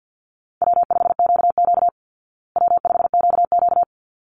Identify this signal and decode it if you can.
Morse code, heard as a keyed tone, about 48 wpm, 700 Hz. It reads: W5QQ W5QQ
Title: Call Signs Repeated 3x - 50wpm